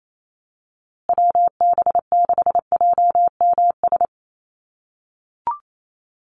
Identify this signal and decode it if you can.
Morse code, heard as a keyed tone, about 28 wpm, 700 Hz. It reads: WB6JMS
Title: USA Call Signs Repeated 2x 28wpm